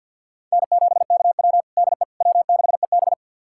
Morse code, heard as a keyed tone, about 53 wpm, 700 Hz. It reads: N7KW DE W6EB
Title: USA Callsign DE Callsign - 4 Characters - 50wpm